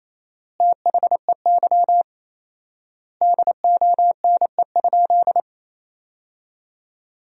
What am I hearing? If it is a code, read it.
Morse code, heard as a keyed tone, about 28 wpm, 700 Hz. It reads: THEY DONE?
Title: 4x Word-Spacing - Sentences from Top 500 Words - 28wpm